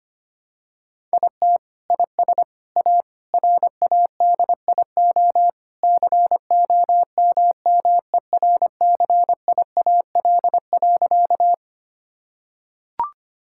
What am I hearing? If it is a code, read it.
Morse code, heard as a keyed tone, about 25 wpm, 700 Hz. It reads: IT IS A RADIO COMMERCIAL.
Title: Sentences from Top 1000 Words - Review Part 5 - 25wpm